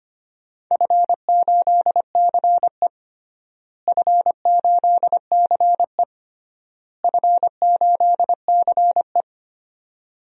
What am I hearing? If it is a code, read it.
Morse code, heard as a keyed tone, about 25 wpm, 700 Hz. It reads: F8CE F8CE F8CE
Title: DXCC-100 Call Signs - 4 Characters - Repeated 3x - 25wpm